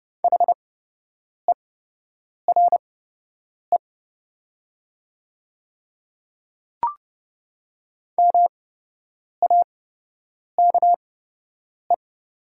Morse code, heard as a keyed tone, about 30 wpm, spaced as for 8 wpm, 700 Hz. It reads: HERE MAKE
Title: 8x Character Spacing - Top 200 Words - 30wpm